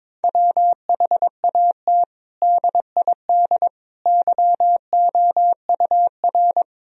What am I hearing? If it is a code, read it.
Morse code, heard as a keyed tone, about 22 wpm, 700 Hz. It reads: WHAT DID YOUR
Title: Sentences from Top 1000 Words - Review Part 3 - 22wpm